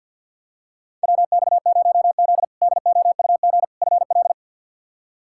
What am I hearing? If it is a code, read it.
Morse code, heard as a keyed tone, about 50 wpm, 700 Hz. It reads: WX0Z DOUG FL
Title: K1USN SST QSO Practice - 50wpm